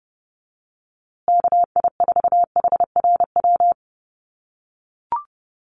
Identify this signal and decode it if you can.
Morse code, heard as a keyed tone, about 30 wpm, 700 Hz. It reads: KI4HRW
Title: USA Call Signs Repeated 3x 30wpm